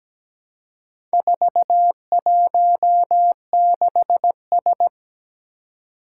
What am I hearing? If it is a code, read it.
Morse code, heard as a keyed tone, about 17 wpm, 700 Hz. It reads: 416S
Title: Four Challenging Letter-Number Combinations 17wpm